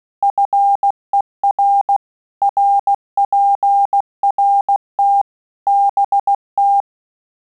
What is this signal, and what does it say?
Morse code, heard as a keyed tone, about 16 wpm, 800 Hz. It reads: FER RPRT BT
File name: QSO CW 75 CPM